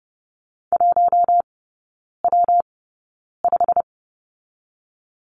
Morse code, heard as a keyed tone, about 30 wpm, 700 Hz. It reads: 1 W 5
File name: Lesson 19e - Letter w - Three Characters 30wpm